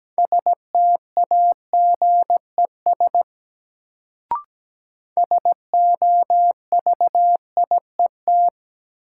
Morse code, heard as a keyed tone, about 17 wpm, 700 Hz. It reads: STAGES SOVIET
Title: Sets of 5 Words Encore - New Word Drill - 17wpm